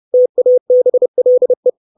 Morse code, heard as a keyed tone, about 30 wpm, 500 Hz. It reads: TABLE
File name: 632 Words 30WPM